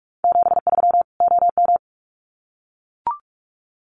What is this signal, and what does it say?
Morse code, heard as a keyed tone, about 44 wpm, 700 Hz. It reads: 73 OM
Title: Common QSO Phrases 45wpm